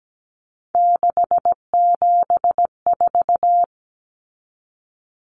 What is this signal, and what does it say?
Morse code, heard as a keyed tone, about 17 wpm, 700 Hz. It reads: 674
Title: Common Cut Numbers - 3-Digit 17wpm